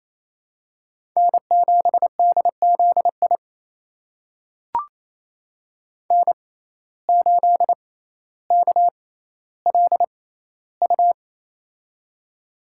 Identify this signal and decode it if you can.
Morse code, heard as a keyed tone, about 28 wpm, spaced as for 14 wpm, 700 Hz. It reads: N7DZI N8KLU
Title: USA Call Signs - 5 Characters - 6x Character Spacing - 28wpm